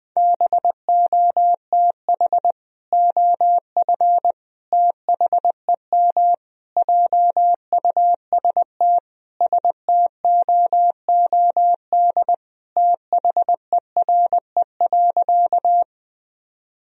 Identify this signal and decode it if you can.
Morse code, heard as a keyed tone, about 20 wpm, 700 Hz. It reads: BOTH OF THEM JUST STOOD THERE.
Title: Sentences from Top 1000 Words - Review Part 2 - 20wpm